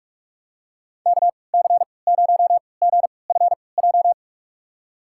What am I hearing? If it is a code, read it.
Morse code, heard as a keyed tone, about 46 wpm, spaced as for 25 wpm, 700 Hz. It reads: KC0GFJ
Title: USA Call Signs - 6 Characters - 3x Character Spacing - 45wpm